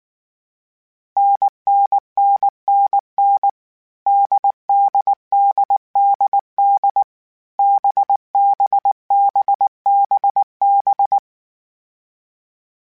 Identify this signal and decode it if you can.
Morse code, heard as a keyed tone, about 19 wpm, 800 Hz. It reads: NNNNN DDDDD BBBBB
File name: Variable Speed